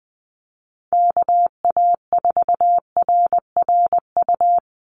Morse code, heard as a keyed tone, about 20 wpm, 700 Hz. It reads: KA4RRU
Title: Call Signs Repeated 3x - 20wpm